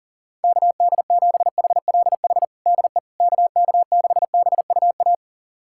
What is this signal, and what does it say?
Morse code, heard as a keyed tone, about 40 wpm, 700 Hz. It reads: KD7HLH DE KK6BUA
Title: USA Callsign DE Callsign - 6 Characters - 40wpm